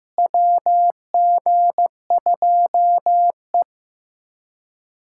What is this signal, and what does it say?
Morse code, heard as a keyed tone, about 16 wpm, 700 Hz. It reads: WG2E
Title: USA Callsign DE Callsign - 4 Characters - 15wpm